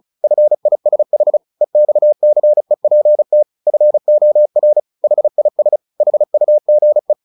Morse code, heard as a keyed tone, about 34 wpm, 600 Hz. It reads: FISH EXCEPT FOR HIS HUGE
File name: the old man and the sea 2 35wpm